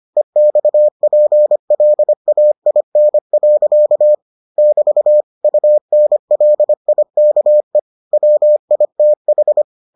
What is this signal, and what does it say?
Morse code, heard as a keyed tone, about 25 wpm, 600 Hz. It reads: EXPLAIN. <BT> UNLIKE WITH